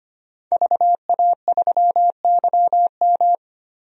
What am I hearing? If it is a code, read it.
Morse code, heard as a keyed tone, about 25 wpm, 700 Hz. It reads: VA3YM
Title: DXCC-100 Call Signs - 5 Characters - 25wpm